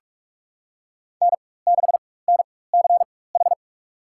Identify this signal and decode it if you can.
Morse code, heard as a keyed tone, about 46 wpm, spaced as for 21 wpm, 700 Hz. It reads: N6NCH
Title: USA Call Signs - 4x Character Spacing - 45wpm